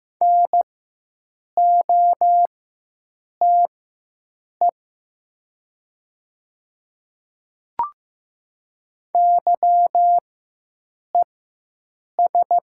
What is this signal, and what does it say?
Morse code, heard as a keyed tone, about 15 wpm, spaced as for 7 wpm, 700 Hz. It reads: NOTE YES
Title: 4x Character Spacing - Top 500 Words - 15wpm